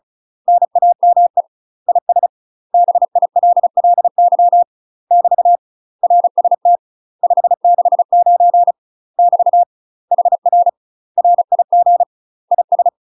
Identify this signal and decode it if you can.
Morse code, heard as a keyed tone, about 36 wpm, 700 Hz. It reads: NAME IS BILLY <BT> RST 569 <BT> HR RIG IS